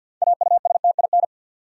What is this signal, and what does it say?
Morse code, heard as a keyed tone, about 53 wpm, 700 Hz. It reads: AUSTIN
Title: USA Common Names - 50wpm